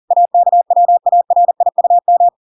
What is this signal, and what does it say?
Morse code, heard as a keyed tone, about 40 wpm, 700 Hz. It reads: AKWARIUM